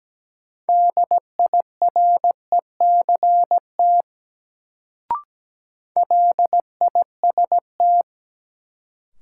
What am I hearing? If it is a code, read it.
Morse code, heard as a keyed tone, about 17 wpm, 700 Hz. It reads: DIRECT LIST
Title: Sentences from Top 400 Words - New Word Drill - 17wpm